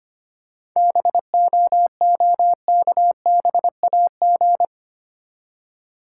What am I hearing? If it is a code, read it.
Morse code, heard as a keyed tone, about 25 wpm, 690 Hz. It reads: BOOKBAG
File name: practice / 25wpm / Easy Words with Hints - 25wpm